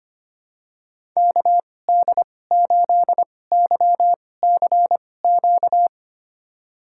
Morse code, heard as a keyed tone, about 25 wpm, spaced as for 18 wpm, 700 Hz. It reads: KD8YCQ
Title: USA Call Signs - 2x Character Spacing - 25wpm